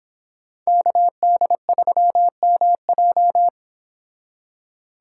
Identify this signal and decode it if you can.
Morse code, heard as a keyed tone, about 30 wpm, 700 Hz. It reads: KD3MJ